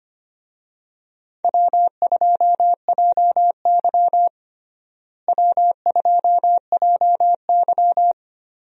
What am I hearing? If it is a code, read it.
Morse code, heard as a keyed tone, about 25 wpm, 700 Hz. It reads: W2JY W2JY
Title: USA Call Signs - 4 Characters - Repeated 3x - 25wpm